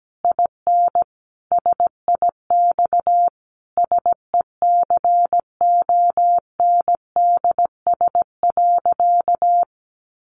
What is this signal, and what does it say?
Morse code, heard as a keyed tone, about 17 wpm, 700 Hz. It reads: IN SIX SECONDS.
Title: A Sound Of Thunder 17wpm